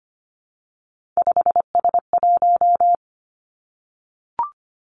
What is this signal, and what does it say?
Morse code, heard as a keyed tone, about 25 wpm, 700 Hz. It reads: 5S1
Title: Three Challenging Letter-Number Combinations 25wpm